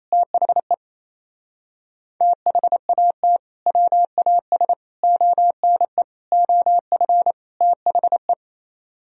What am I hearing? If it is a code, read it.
Morse code, heard as a keyed tone, about 27 wpm, 700 Hz. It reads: THE THAT WAS ONE OF THE
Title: Speed-Racing - Sets of 5 Words 28wpm